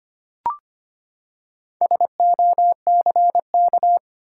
Morse code, heard as a keyed tone, about 25 wpm, 700 Hz. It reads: SOCK
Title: Four Letter Words - 25wpm